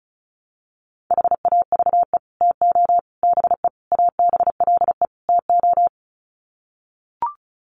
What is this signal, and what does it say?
Morse code, heard as a keyed tone, about 34 wpm, 700 Hz. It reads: HAVE TO BE ABLE TO
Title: Sets of 5 Words 35wpm